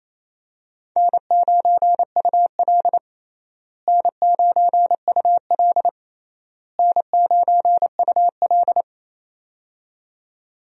Morse code, heard as a keyed tone, about 28 wpm, 700 Hz. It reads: N9UL N9UL N9UL
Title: USA Call Signs - 4 Characters - Repeated 3x - 28wpm